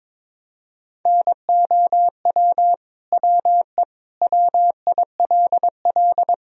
Morse code, heard as a keyed tone, about 22 wpm, 700 Hz. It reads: NOW WE WILL
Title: Sentences from Top 1000 Words - Review Part 1 - 22wpm